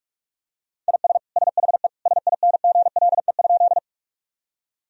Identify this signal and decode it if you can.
Morse code, heard as a keyed tone, about 46 wpm, 700 Hz. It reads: IS SHE SINGLE?
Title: Sentences from Top 1000 Words - Review Part 4 - 45wpm